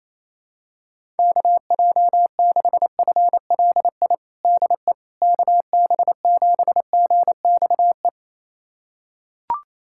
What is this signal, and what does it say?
Morse code, heard as a keyed tone, about 28 wpm, 700 Hz. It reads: KJ6FLI DE KB7GXE
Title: USA Callsign DE Callsign - 6 Characters - 28wpm